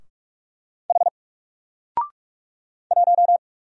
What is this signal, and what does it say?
Morse code, heard as a keyed tone, about 44 wpm, 700 Hz. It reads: H 1
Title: Single Character (Letter:Number:Punctuation:Prosign) 45wpm